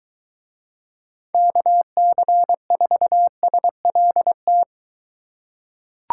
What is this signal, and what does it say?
Morse code, heard as a keyed tone, about 23 wpm, 700 Hz. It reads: KC4SLT